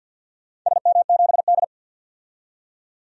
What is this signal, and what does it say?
Morse code, heard as a keyed tone, about 52 wpm, 700 Hz. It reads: SM7D
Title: DXCC-100 Call Signs - 4 Characters - 50wpm